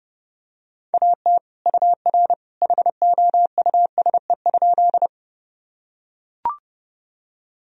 Morse code, heard as a keyed tone, about 30 wpm, 700 Hz. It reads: AT UR HOUSE?
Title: Sentences from Top 200 Words and Abbreviations 30wpm